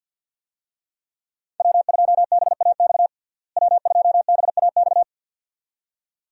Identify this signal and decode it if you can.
Morse code, heard as a keyed tone, about 48 wpm, 700 Hz. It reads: W2BAX W2BAX
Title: USA Call Signs - 5 Characters - Repeated 2x - 50wpm